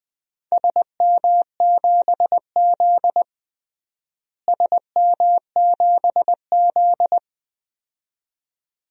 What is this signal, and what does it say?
Morse code, heard as a keyed tone, about 20 wpm, 700 Hz. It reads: SM7Z SM7Z
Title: DXCC-100 Call Signs - 4 Characters - Repeated 2x - 20wpm